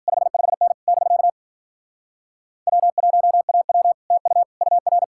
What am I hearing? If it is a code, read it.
Morse code, heard as a keyed tone, about 53 wpm, 700 Hz. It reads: 55N <BK> W1AW TU UR